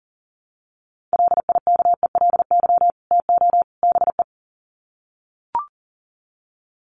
Morse code, heard as a keyed tone, about 40 wpm, 700 Hz. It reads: LIKELY TO BE
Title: Sets of 3 Words 40wpm